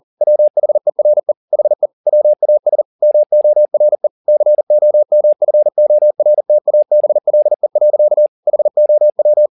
Morse code, heard as a keyed tone, about 40 wpm, 600 Hz. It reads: WHERE HE WAS MORE COMFORTABLE. HOW